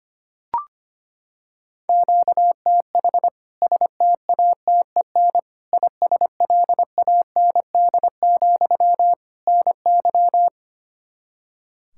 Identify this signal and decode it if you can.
Morse code, heard as a keyed tone, about 25 wpm, 700 Hz. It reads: QTH STATEN ISLAND, NY
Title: K1USN SST QSO Practice 25wpm